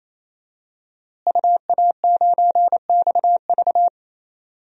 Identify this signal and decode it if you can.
Morse code, heard as a keyed tone, about 28 wpm, 700 Hz. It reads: UA9XV
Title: DXCC-100 Call Signs - 5 Characters - Repeated 3x - 28wpm